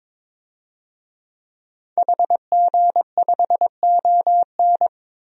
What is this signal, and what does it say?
Morse code, heard as a keyed tone, about 22 wpm, 700 Hz. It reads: HG5ON
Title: DXCC-100 Call Signs - 5 Characters - 22wpm